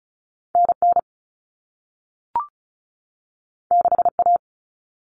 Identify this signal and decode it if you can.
Morse code, heard as a keyed tone, about 36 wpm, 690 Hz. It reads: NN 6A
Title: All Cut Numbers - 2-Digit 35wpm